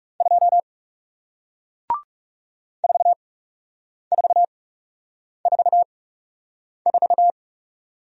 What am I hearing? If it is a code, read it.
Morse code, heard as a keyed tone, about 40 wpm, spaced as for 8 wpm, 700 Hz. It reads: 2 4444